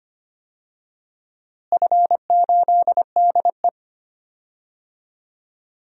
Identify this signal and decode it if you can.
Morse code, heard as a keyed tone, about 25 wpm, 700 Hz. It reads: F8DE
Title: DXCC-100 Call Signs - 4 Characters - Repeated 3x - 25wpm